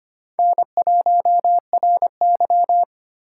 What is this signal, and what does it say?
Morse code, heard as a keyed tone, about 31 wpm, 700 Hz. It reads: N1RY
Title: Speed Racing - USA 1x2 Call Signs 25wpm